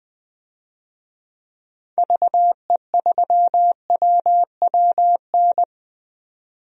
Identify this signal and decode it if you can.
Morse code, heard as a keyed tone, about 20 wpm, 700 Hz. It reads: VE3WWN
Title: DXCC-100 Call Signs - 6 Characters - 20wpm